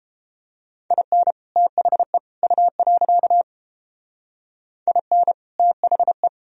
Morse code, heard as a keyed tone, about 31 wpm, 700 Hz. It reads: IN THE U. IN THE